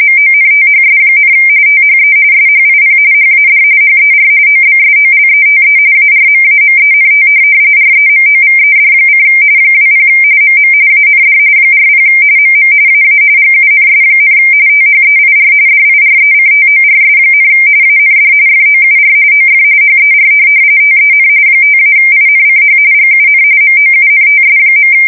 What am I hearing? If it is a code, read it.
RTTY, about 45.45 baud. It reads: MONTHS, THE SHIP HAD HER OBSOLETE ARMAMENT REMOVED: 20 MM AND 40 MM AA GUNS, THEIR DIRECTORS, AND FOUR OF HER TEN 5-INCH GUN MOUNTS.
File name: fsk170 / USS-Missouri